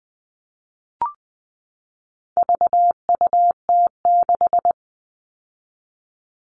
Morse code, heard as a keyed tone, about 20 wpm, 700 Hz. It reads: VUT6
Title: All Cut Numbers - 4-Digit 20wpm